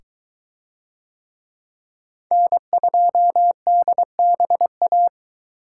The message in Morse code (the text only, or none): N2DBA